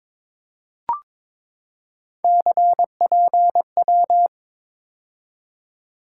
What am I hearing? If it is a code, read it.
Morse code, heard as a keyed tone, about 22 wpm, 700 Hz. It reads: CPW
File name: practice / 22wpm / Three Letter-Number Combinations - 22wpm